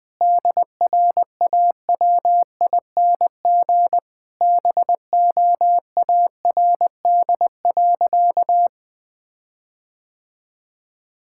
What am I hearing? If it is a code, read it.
Morse code, heard as a keyed tone, about 20 wpm, 700 Hz. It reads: DRAWING BOARD.
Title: Sentences with 7-Word Phrases - Spoken Beginning - 20wpm